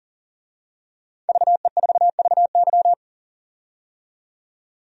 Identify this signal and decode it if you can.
Morse code, heard as a keyed tone, about 40 wpm, 700 Hz. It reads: VE4VY
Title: DXCC-100 Call Signs - 5 Characters - 40wpm